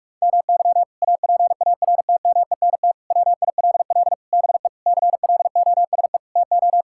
Morse code, heard as a keyed tone, about 46 wpm, 700 Hz. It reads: MY APARTMENT WILL BE CLOSE TO
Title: Sentences from Top 1000 Words - Review Part 6 - 45wpm